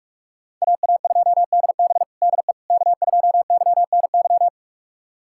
Morse code, heard as a keyed tone, about 43 wpm, 700 Hz. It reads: AA2DB DE K2YNY
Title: USA Callsign DE Callsign - 5 Characters - 45wpm